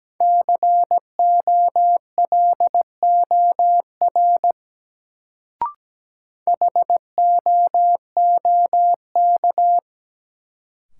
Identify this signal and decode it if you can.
Morse code, heard as a keyed tone, about 17 wpm, 700 Hz. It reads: COLOR HOOK
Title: Sentences from Top 1600 Words - New Word Drill Alt - 17wpm